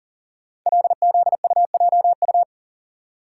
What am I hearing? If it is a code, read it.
Morse code, heard as a keyed tone, about 38 wpm, 700 Hz. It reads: LZUJU
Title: Code Groups - 5 Letters - 40wpm